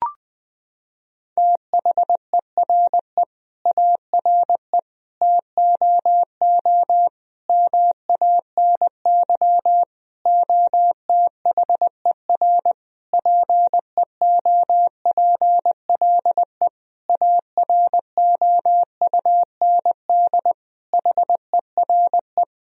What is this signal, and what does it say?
Morse code, heard as a keyed tone, about 21 wpm, 700 Hz. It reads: THERE ARE TOO MANY OTHER PEOPLE AROUND HERE
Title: Sentences from Top 1000 Words - Review Part 3 - 20wpm